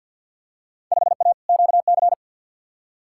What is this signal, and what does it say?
Morse code, heard as a keyed tone, about 51 wpm, 700 Hz. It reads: 5A QC
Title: Field Day Exchanges Encore - 50wpm